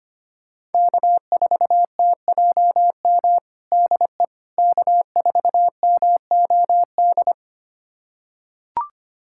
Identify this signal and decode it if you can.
Morse code, heard as a keyed tone, about 25 wpm, 700 Hz. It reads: K4TJM DE K4MOD
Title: USA Callsign DE Callsign - 5 Characters - 25wpm